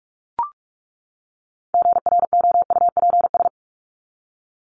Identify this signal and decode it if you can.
Morse code, heard as a keyed tone, about 44 wpm, 700 Hz. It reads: GROUPS
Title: ES:S Suffix - Top 100 Words with Base Words 45wpm